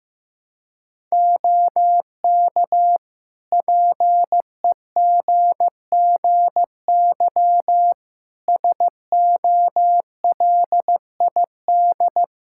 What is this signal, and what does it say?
Morse code, heard as a keyed tone, about 16 wpm, 700 Hz. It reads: OK PEGGY SOLID
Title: Practice QSO - Exchange 3 - More Info - 15wpm